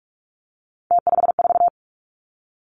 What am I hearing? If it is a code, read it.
Morse code, heard as a keyed tone, about 46 wpm, 700 Hz. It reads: T54
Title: Three Challenging Letter-Number Combinations 45wpm